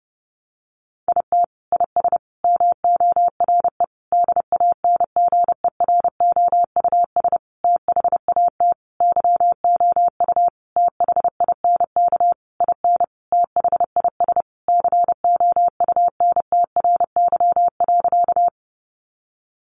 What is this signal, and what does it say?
Morse code, heard as a keyed tone, about 30 wpm, 700 Hz. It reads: IT IS MORE DANGEROUS THAT YOU THINK IN THIS COUNTRY.
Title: Sentences from Top 1200 Words 30wpm